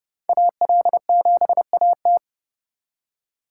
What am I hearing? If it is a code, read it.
Morse code, heard as a keyed tone, about 27 wpm, 700 Hz. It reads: AL7AT